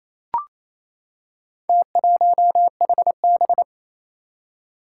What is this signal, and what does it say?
Morse code, heard as a keyed tone, about 27 wpm, 700 Hz. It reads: T1HB
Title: Four Challenging Letter-Number Combinations 28wpm